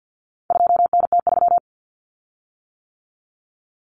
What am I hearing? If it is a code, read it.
Morse code, heard as a keyed tone, about 50 wpm, 700 Hz. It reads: 2NT3
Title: Common Cut Numbers - 4-Digit 50wpm